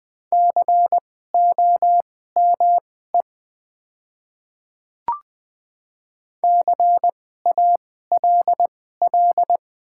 Morse code, heard as a keyed tone, about 20 wpm, spaced as for 14 wpm, 700 Hz. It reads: COME CALL
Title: 2x Character Spacing - Top 100 Words - 20wpm